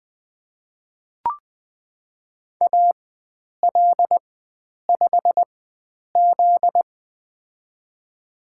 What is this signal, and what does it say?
Morse code, heard as a keyed tone, about 20 wpm, spaced as for 9 wpm, 700 Hz. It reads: AL5Z
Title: USA Call Signs - 4 Characters - 4x Character Spacing - 20wpm